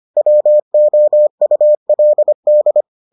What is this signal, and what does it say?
Morse code, heard as a keyed tone, about 25 wpm, 600 Hz. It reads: WOULD